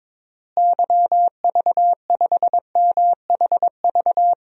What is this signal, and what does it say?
Morse code, heard as a keyed tone, about 22 wpm, 700 Hz. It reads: YV5MHV
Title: DXCC-100 Call Signs - 6 Characters - Repeated 3x - 22wpm